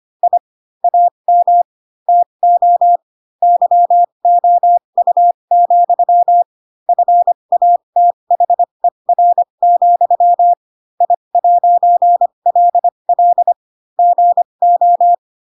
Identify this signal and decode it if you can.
Morse code, heard as a keyed tone, about 25 wpm, spaced as for 22 wpm, 700 Hz. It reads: I AM TO YOU, FATHER, I'LL GO